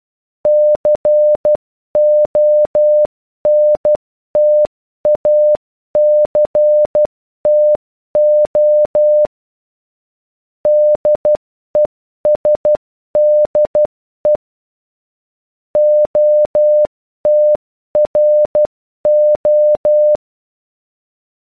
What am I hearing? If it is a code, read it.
Morse code, heard as a keyed tone, about 12 wpm, 600 Hz. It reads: CONTACTO DESDE OTRO